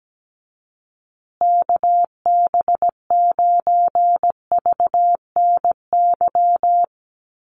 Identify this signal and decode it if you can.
Morse code, heard as a keyed tone, about 17 wpm, 700 Hz. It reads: KB9VNY
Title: USA Call Signs Repeated 3x 17wpm